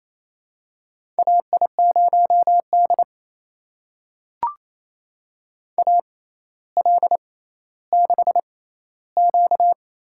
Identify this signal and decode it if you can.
Morse code, heard as a keyed tone, about 28 wpm, spaced as for 14 wpm, 700 Hz. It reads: AI0D AL6Q
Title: USA Call Signs - 4 Characters - 6x Character Spacing - 28wpm